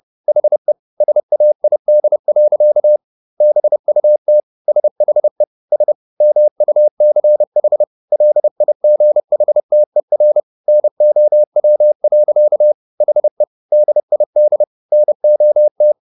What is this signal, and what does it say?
Morse code, heard as a keyed tone, about 30 wpm, 600 Hz. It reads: HE SAID. BUT SHE S MUCH LIGHTER NOW. HE DID NOT